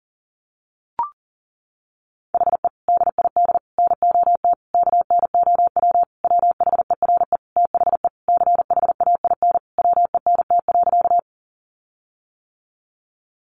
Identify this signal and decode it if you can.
Morse code, heard as a keyed tone, about 40 wpm, 700 Hz. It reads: HE DID NOT KNOW WHERE THE CHAIN WENT.
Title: Sentences from Top 1500 Words 40wpm